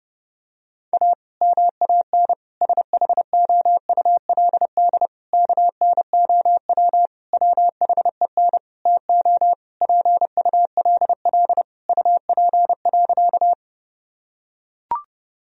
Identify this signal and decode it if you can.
Morse code, heard as a keyed tone, about 32 wpm, 700 Hz. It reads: A MAN SHOULD KNOW WHEN TO PULL UP.
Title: Sentences from Top 1000 Words - Review Part 2 - 30wpm